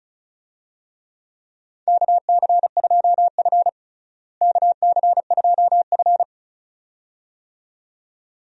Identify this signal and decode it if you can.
Morse code, heard as a keyed tone, about 36 wpm, 700 Hz. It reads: KC2F KC2F
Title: USA Call Signs - 4 Characters - Repeated 2x - 35wpm